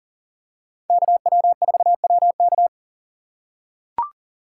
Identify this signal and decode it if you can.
Morse code, heard as a keyed tone, about 40 wpm, 700 Hz. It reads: KW4WK
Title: USA Call Signs - 5 Characters - 6x Character Spacing - 40wpm